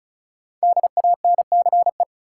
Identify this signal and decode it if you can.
Morse code, heard as a keyed tone, about 36 wpm, 700 Hz. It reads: DANCE
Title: Sentences from Top 700 Words - New Word Drill - 35wpm